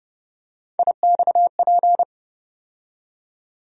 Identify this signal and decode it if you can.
Morse code, heard as a keyed tone, about 30 wpm, 700 Hz. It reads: IXP
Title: USA Call Sign Postfixes - 3 Letters 30wpm